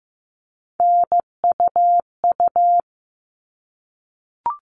Morse code, heard as a keyed tone, about 15 wpm, 700 Hz. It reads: NUU
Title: All Cut Numbers - 3-Digit 15wpm